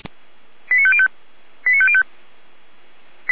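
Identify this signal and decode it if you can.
Five-tone selective call (ZVEI), heard as quick runs of five tones, tones of about 70 ms. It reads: 89505 89505
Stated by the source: ZVEI five-tone call received on SDR.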